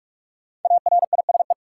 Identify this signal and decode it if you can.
Morse code, heard as a keyed tone, about 46 wpm, 700 Hz. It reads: ARISE
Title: Five Letter Words - 45wpm